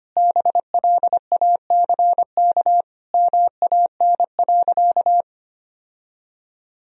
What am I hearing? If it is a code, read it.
Morse code, heard as a keyed tone, about 25 wpm, 700 Hz. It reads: BLACK MAN.
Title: Sentences from Top 1000 Words - Review Part 2 - 25wpm